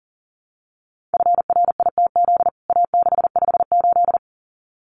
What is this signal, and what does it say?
Morse code, heard as a keyed tone, about 37 wpm, 700 Hz. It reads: FRITZ A658
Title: CWT Exchange 40wpm